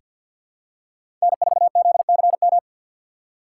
Morse code, heard as a keyed tone, about 47 wpm, 700 Hz. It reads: N4ZCM
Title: USA Call Signs - 5 Characters - 4x Character Spacing - 50wpm